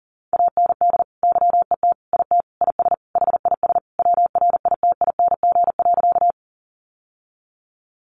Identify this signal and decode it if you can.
Morse code, heard as a keyed tone, about 40 wpm, 700 Hz. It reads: AND YET IT IS HIS WRITING.
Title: Sentences from Top 800 Words 40wpm